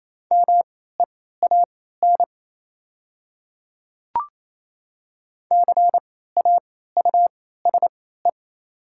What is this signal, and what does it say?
Morse code, heard as a keyed tone, about 28 wpm, spaced as for 16 wpm, 700 Hz. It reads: MEAN CAUSE
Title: 3x Character Spacing - Top 300 Words - 28wpm